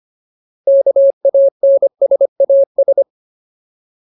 Morse code, heard as a keyed tone, about 25 wpm, 550 Hz. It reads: KANSAS
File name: Ultimate Word List / 25WPM Ultimate